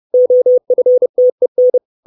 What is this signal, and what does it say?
Morse code, heard as a keyed tone, about 30 wpm, 500 Hz. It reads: OFTEN
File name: 632 Words 30WPM